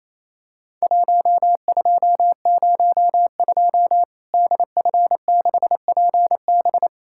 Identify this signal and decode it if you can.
Morse code, heard as a keyed tone, about 28 wpm, 700 Hz. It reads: 1202 DF6PB
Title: Worked All Europe DX Contest - QTC Exchange - 28wpm